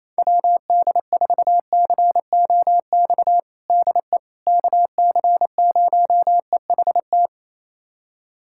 Morse code, heard as a keyed tone, about 28 wpm, 700 Hz. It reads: WD4COX DE KC0EHT
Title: USA Callsign DE Callsign 28wpm